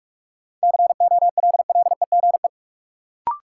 Morse code, heard as a keyed tone, about 44 wpm, 700 Hz. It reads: COLLEGE
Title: Sentences from Top 1300 Words - New Word Drill - 45wpm